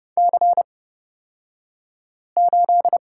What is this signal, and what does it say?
Morse code, heard as a keyed tone, about 30 wpm, spaced as for 5 wpm, 700 Hz. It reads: C8
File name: Lesson 36d - BT - Two Characters 30wpm